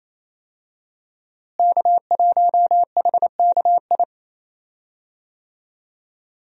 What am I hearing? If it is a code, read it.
Morse code, heard as a keyed tone, about 28 wpm, 700 Hz. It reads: K1HKI
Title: USA Call Signs - 5 Characters - Repeated 2x - 28wpm